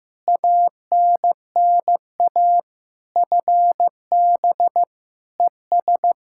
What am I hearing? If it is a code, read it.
Morse code, heard as a keyed tone, about 15 wpm, 700 Hz. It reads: ANNA FB ES
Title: Practice QSO - Exchange 2 - Info - 15wpm